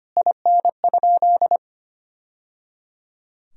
Morse code, heard as a keyed tone, about 25 wpm, 700 Hz. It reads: IN?
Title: Sentences from Top 1000 Words - Review Part 2 - 25wpm